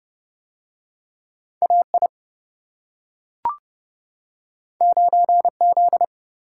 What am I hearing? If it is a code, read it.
Morse code, heard as a keyed tone, about 29 wpm, 700 Hz. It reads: AI 9Z